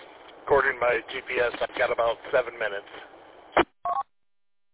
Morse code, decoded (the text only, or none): none